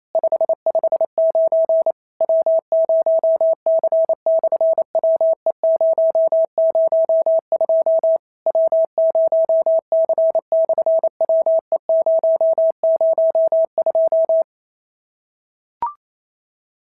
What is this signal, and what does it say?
Morse code, heard as a keyed tone, about 27 wpm, 650 Hz. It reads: 559 W0C/WE002 W0C/WE002
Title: SOTA QSO Practice 28wpm